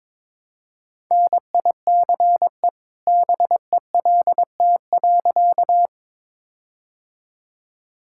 Morse code, heard as a keyed tone, about 22 wpm, 690 Hz. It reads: NICE BELT.
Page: morse-code-ninja-practice-22wpm